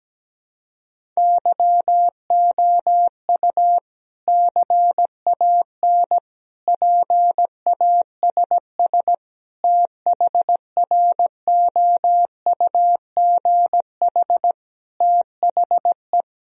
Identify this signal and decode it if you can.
Morse code, heard as a keyed tone, about 17 wpm, 700 Hz. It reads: YOU CAN PASS THROUGH THE
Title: Sentences from Top 1000 Words - Review Part 1 - 17wpm